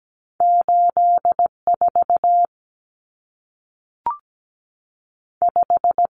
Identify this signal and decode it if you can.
Morse code, heard as a keyed tone, about 17 wpm, 700 Hz. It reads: 8 4 5
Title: Numbers - 2-Digit 17wpm